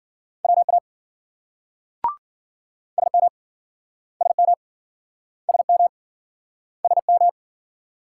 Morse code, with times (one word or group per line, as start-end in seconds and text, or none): 0.4-0.8 WA
3.0-3.3 SM
4.2-4.5 SM
5.5-5.9 SM
6.8-7.3 SM